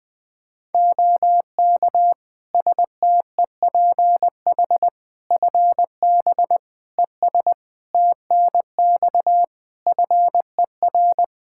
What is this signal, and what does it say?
Morse code, heard as a keyed tone, about 20 wpm, 700 Hz. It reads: OK STEPH FB ES TNX FER
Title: Practice QSO - Exchange 2 - Info - 20wpm